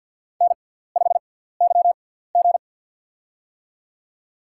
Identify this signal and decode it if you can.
Morse code, heard as a keyed tone, about 53 wpm, spaced as for 17 wpm, 700 Hz. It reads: N5YG
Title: USA Call Signs - 4 Characters - 6x Character Spacing - 50wpm